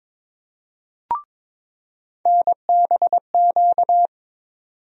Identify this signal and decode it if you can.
Morse code, heard as a keyed tone, about 22 wpm, 700 Hz. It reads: NBQ
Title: USA Call Sign Postfixes - 3 Letters 22wpm